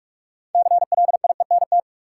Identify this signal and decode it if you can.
Morse code, heard as a keyed tone, about 43 wpm, 700 Hz. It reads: CLIENT
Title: Six Letter Words - 45wpm